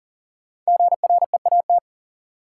Morse code, heard as a keyed tone, about 40 wpm, 700 Hz. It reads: GREAT
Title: Sentences from Top 200 Words - New Word Drill - 40wpm